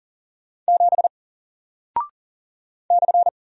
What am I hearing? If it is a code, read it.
Morse code, heard as a keyed tone, about 40 wpm, 700 Hz. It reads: 7 /
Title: Single Character (Letter:Number:Punctuation:Prosign) Alt 40wpm